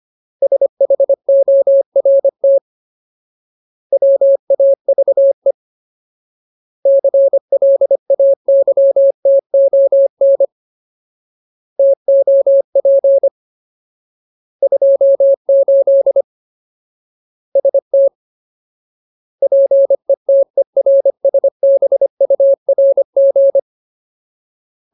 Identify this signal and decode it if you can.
Morse code, heard as a keyed tone, about 25 wpm, 550 Hz. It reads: SHORT WAVE CLAYTON TOP 28 ST PETERSBURG
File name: Ultimate Word List / 25WPM Ultimate